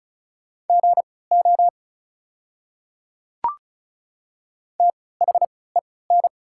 Morse code, heard as a keyed tone, about 36 wpm, spaced as for 20 wpm, 700 Hz. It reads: GO THEN
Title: 3x Character Spacing - Top 100 Words - 35wpm